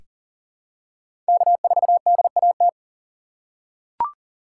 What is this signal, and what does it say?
Morse code, heard as a keyed tone, about 37 wpm, 700 Hz. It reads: K4DAT
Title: USA Call Signs - 2x Character Spacing - 40wpm